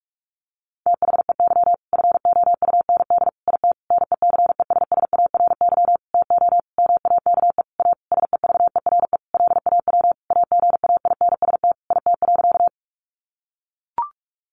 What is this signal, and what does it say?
Morse code, heard as a keyed tone, about 44 wpm, 700 Hz. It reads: THEY FOUND IT NECESSARY TO MAKE A SEVERE LAW AGAINST IT.
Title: Sentences from Top 1400 Words 45wpm